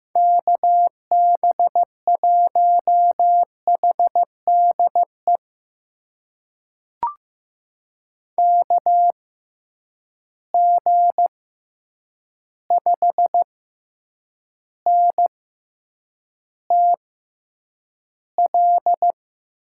Morse code, heard as a keyed tone, about 15 wpm, spaced as for 8 wpm, 700 Hz. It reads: KB1HDE KG5NTL
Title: USA Call Signs - 6 Characters - 6x Character Spacing - 15wpm